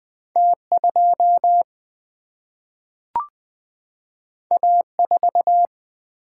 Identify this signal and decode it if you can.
Morse code, heard as a keyed tone, about 20 wpm, 700 Hz. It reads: T2 A4
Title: Single-Letter Single-Number 20wpm